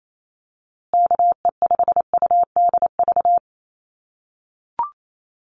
Morse code, heard as a keyed tone, about 28 wpm, 700 Hz. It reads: KE5UDV
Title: USA Call Signs Repeated 2x 28wpm